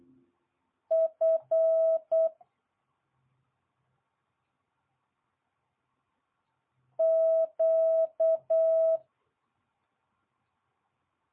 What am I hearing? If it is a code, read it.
Morse code, heard as a keyed tone, about 8 wpm, 650 Hz. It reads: F Q
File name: jh8cbh / 2022cw